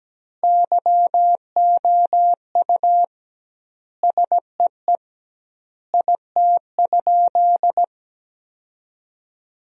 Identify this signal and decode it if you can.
Morse code, heard as a keyed tone, about 17 wpm, 700 Hz. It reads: YOU SEE IT?
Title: 2x Word-Spacing - Sentences from Top 100 Words - 17wpm